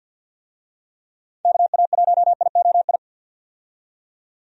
Morse code, heard as a keyed tone, about 51 wpm, 700 Hz. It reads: KA1IOI
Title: USA Call Signs - 6 Characters - 50wpm